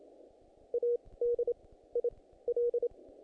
Morse code, heard as a keyed tone, about 27 wpm, 480 Hz. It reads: A D I L
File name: contests / naqp-cw-k5tr-op-k5pi- -2000